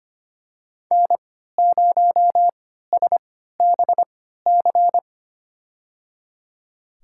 Morse code, heard as a keyed tone, about 25 wpm, spaced as for 14 wpm, 700 Hz. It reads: N0SBC
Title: USA Call Signs - 5 Characters - 3x Character Spacing - 25wpm